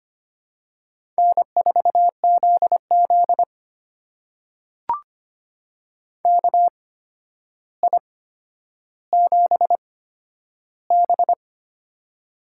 Morse code, heard as a keyed tone, about 25 wpm, spaced as for 11 wpm, 700 Hz. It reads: N4ZZ KI7B
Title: USA Call Signs - 4 Characters - 8x Character Spacing - 25wpm